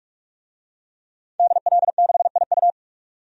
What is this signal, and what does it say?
Morse code, heard as a keyed tone, about 44 wpm, 700 Hz. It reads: DL6IU
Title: DXCC-100 Call Signs - 5 Characters - 45wpm